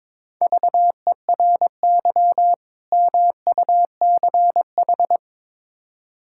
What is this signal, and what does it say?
Morse code, heard as a keyed tone, about 22 wpm, 700 Hz. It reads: VERY MUCH
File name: practice / 22wpm / Build A Phrase - 5-Word Phrases - 22wpm